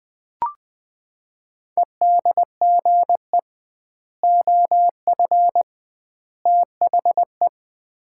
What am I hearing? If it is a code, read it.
Morse code, heard as a keyed tone, about 20 wpm, 700 Hz. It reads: EDGE OF THE
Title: 2x Word-Spacing - Sets of 3 Words - 20wpm